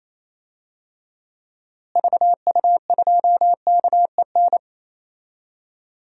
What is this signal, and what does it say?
Morse code, heard as a keyed tone, about 28 wpm, 700 Hz. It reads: VU2KEN
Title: DXCC-100 Call Signs - 6 Characters - Repeated 3x - 28wpm